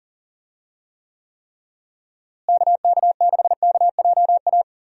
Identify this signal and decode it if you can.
Morse code, heard as a keyed tone, about 40 wpm, 700 Hz. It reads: KK6KJA
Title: USA Call Signs - 6 Characters - Repeated 3x - 40wpm